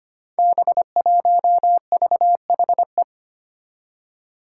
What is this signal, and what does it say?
Morse code, heard as a keyed tone, about 25 wpm, 700 Hz. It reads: B1VHE
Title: Five Challenging Letter-Number Combinations 25wpm